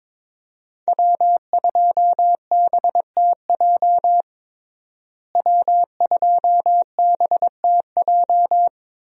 Morse code, heard as a keyed tone, about 21 wpm, 700 Hz. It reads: W2BTJ W2BTJ
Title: USA Call Signs - 5 Characters - Repeated 2x - 22wpm